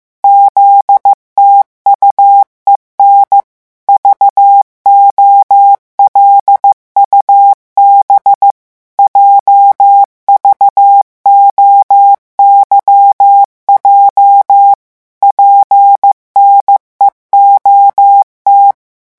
Morse code, heard as a keyed tone, about 15 wpm, 800 Hz. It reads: ZTUEN VOLUB JVOYJ PNEOT
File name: morse mp3 / revision8-vitesse 14 mots